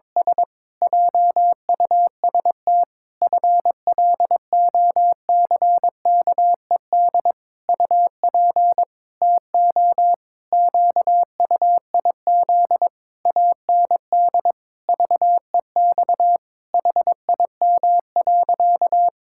Morse code, heard as a keyed tone, about 22 wpm, 700 Hz. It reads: S JUST FLOCKED UP TO QUIZ AND VEX HIM.
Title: Pangram Challenge v2 - 22wpm